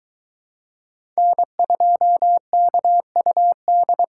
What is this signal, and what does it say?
Morse code, heard as a keyed tone, about 20 wpm, 700 Hz. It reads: N2KUD